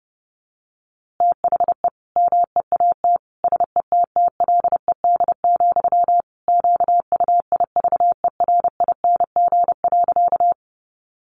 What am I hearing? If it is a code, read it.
Morse code, heard as a keyed tone, about 30 wpm, 700 Hz. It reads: THE MEAT SETTLED, QUIVERING.
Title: A Sound Of Thunder 30wpm